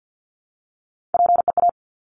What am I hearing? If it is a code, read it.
Morse code, heard as a keyed tone, about 50 wpm, 700 Hz. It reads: PEA
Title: Top Three Letter Combinations in Words 50wpm